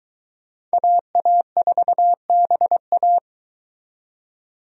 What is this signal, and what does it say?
Morse code, heard as a keyed tone, about 22 wpm, 700 Hz. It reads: AA4BA